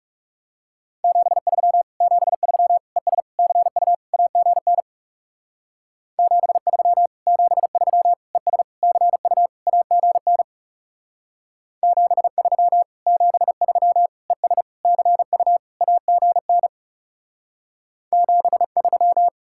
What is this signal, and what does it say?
Morse code, heard as a keyed tone, about 35 wpm, 700 Hz. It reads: 73 73 ES CU AGN 73 73 ES CU AGN 73 73 ES CU AGN 73
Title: Speed-Racing Common QSO Phrases - 30wpm